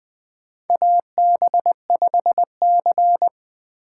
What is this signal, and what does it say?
Morse code, heard as a keyed tone, about 20 wpm, 700 Hz. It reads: AB5C
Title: K1USN SST QSO Practice - 20wpm